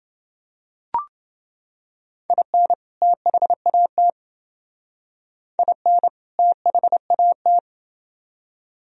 Morse code, heard as a keyed tone, about 28 wpm, 700 Hz. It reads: IN THAT IN THAT